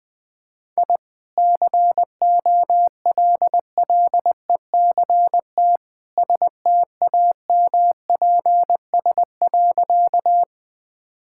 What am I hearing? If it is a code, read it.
Morse code, heard as a keyed tone, about 20 wpm, 700 Hz. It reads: I COLLECT STAMPS.
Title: Easy Sentences with Hints - 20wpm